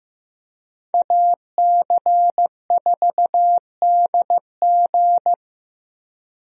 Morse code, heard as a keyed tone, about 15 wpm, 700 Hz. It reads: AC4DG
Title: USA Call Signs - 5 Characters - Repeated 3x - 15wpm